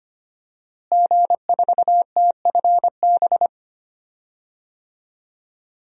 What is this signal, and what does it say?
Morse code, heard as a keyed tone, about 25 wpm, 700 Hz. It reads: G4TFB
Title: DXCC-100 Call Signs - 5 Characters - Repeated 3x - 25wpm